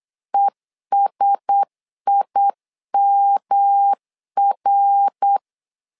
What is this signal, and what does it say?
Morse code, heard as a keyed tone, about 8 wpm, 800 Hz. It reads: ESIMR
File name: LEZIONI / 4 Q1S 8wpm 2710